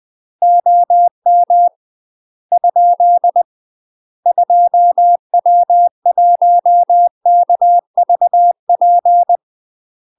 Morse code, heard as a keyed tone, about 20 wpm, 700 Hz. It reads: OM ? 2W1KVP
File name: UK MASTER QSOs TRAINING / 20WPM / UK MASTER QSOs 0053